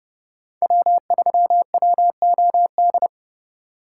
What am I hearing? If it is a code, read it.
Morse code, heard as a keyed tone, about 29 wpm, 700 Hz. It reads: W3WOD
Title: USA Call Signs - 6x Character Spacing - 30wpm